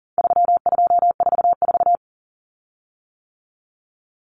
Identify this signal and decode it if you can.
Morse code, heard as a keyed tone, about 40 wpm, 700 Hz. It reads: 3244
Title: Common Cut Numbers - 4-Digit 40wpm